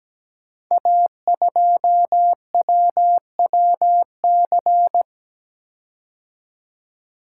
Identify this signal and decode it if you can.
Morse code, heard as a keyed tone, about 17 wpm, 700 Hz. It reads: A2WWC
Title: Code Groups - 5 Letter:Numbers - 17wpm